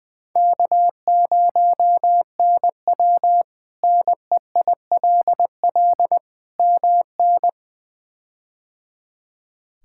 Morse code, heard as a keyed tone, about 21 wpm, 700 Hz. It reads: K0NW NEILL MN
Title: K1USN SST QSO Practice - 20wpm